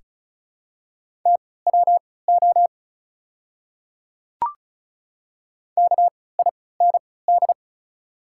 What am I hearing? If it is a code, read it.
Morse code, heard as a keyed tone, about 34 wpm, spaced as for 20 wpm, 690 Hz. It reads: TWO KIND
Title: 3x Character Spacing - Top 300 Words - 35wpm